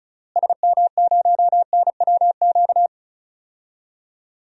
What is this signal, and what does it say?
Morse code, heard as a keyed tone, about 36 wpm, 700 Hz. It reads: SM0NWQ
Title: DXCC-100 Call Signs - 6 Characters - Repeated 3x - 35wpm